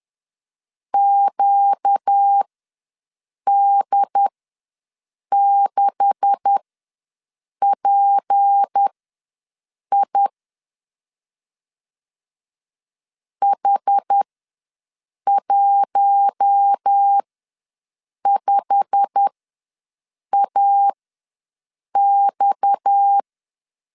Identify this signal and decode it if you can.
Morse code, heard as a keyed tone, about 10 wpm, spaced as for 6 wpm, 800 Hz. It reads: QD6PI H15AX
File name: LEZIONI / 9 286 ws 10wpm 4032